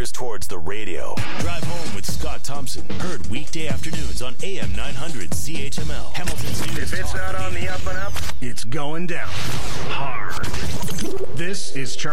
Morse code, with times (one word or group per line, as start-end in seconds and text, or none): none